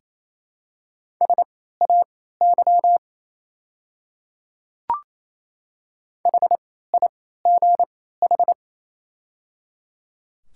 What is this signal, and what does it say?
Morse code, heard as a keyed tone, about 27 wpm, spaced as for 16 wpm, 700 Hz. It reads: SAY HIGH
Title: 3x Character Spacing - Top 200 Words - 28wpm